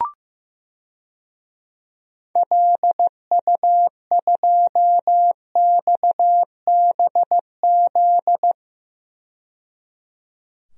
Morse code, heard as a keyed tone, about 15 wpm, 700 Hz. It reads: LU2XBZ
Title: DXCC-100 Call Signs - 6 Characters - 15wpm